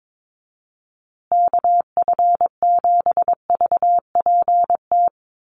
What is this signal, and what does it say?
Morse code, heard as a keyed tone, about 22 wpm, 700 Hz. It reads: KF7VPT
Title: USA Call Signs Repeated 3x 22wpm